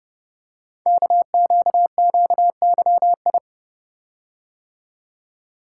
Morse code, heard as a keyed tone, about 30 wpm, 700 Hz. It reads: KQQYI
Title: Code Groups - 5 Letter:Numbers - 30wpm